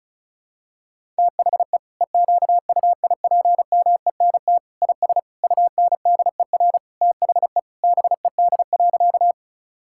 Morse code, heard as a keyed tone, about 36 wpm, 700 Hz. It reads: THE EQUIPMENT IS UNDER THE BED.
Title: Sentences from Top 1000 Words - Review Part 5 - 35wpm